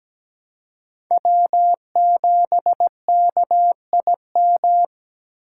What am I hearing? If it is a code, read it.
Morse code, heard as a keyed tone, about 17 wpm, 700 Hz. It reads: W7KIM
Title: USA Call Signs - 5 Characters - 6x Character Spacing - 17wpm